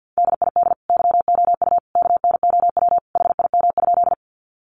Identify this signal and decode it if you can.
Morse code, heard as a keyed tone, about 50 wpm, 700 Hz. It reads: DID YOU KNOW HIM?
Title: Sentences from Top 100 Words 50wpm